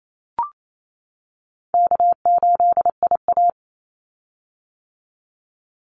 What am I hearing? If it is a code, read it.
Morse code, heard as a keyed tone, about 28 wpm, 700 Hz. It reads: K8IA
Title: Easy Call Signs - 28wpm